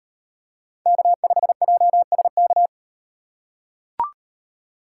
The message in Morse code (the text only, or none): K5JSK